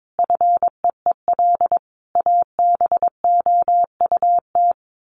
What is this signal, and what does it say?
Morse code, heard as a keyed tone, about 22 wpm, 700 Hz. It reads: FEEL ABOUT
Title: Sets of 5 Words 22wpm